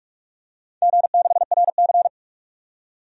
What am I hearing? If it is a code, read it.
Morse code, heard as a keyed tone, about 41 wpm, 700 Hz. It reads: G6RC
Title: DXCC-100 Call Signs - 4 Characters - 45wpm